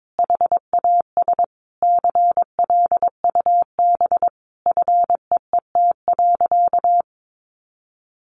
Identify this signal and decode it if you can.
Morse code, heard as a keyed tone, about 22 wpm, 700 Hz. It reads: HAS CLUB FEET.
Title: Sentences from Top 1200 Words 22wpm